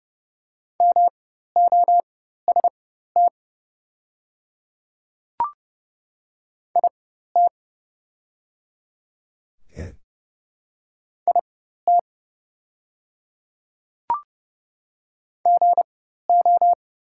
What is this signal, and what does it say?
Morse code, heard as a keyed tone, about 30 wpm, spaced as for 14 wpm, 700 Hz. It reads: MOST IT IT GO
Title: 4x Character Spacing - Top 100 Words - 30wpm